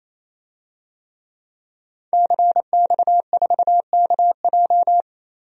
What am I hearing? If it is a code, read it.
Morse code, heard as a keyed tone, about 28 wpm, 700 Hz. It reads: CX4KJ
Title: DXCC-100 Call Signs - 5 Characters - 28wpm